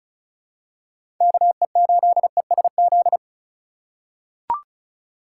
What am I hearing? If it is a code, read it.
Morse code, heard as a keyed tone, about 36 wpm, 700 Hz. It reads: KE8ESZ
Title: USA Call Signs - 6 Characters - 3x Character Spacing - 35wpm